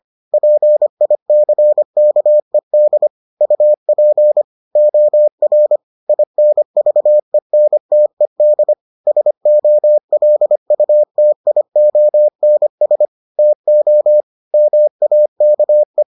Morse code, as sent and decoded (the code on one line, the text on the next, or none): .--. .. -.-. -.- . -.. / ..- .--. / --- .-. / .. -. ...- . -. - . -.. / ... --- .-.. ..- - .. --- -. ... / - --- / -- .- -.- .
PICKED UP OR INVENTED SOLUTIONS TO MAKE